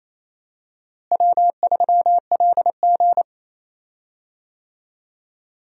Morse code, heard as a keyed tone, about 28 wpm, 700 Hz. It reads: W3LG
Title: USA Call Signs - 4 Characters - Repeated 3x - 28wpm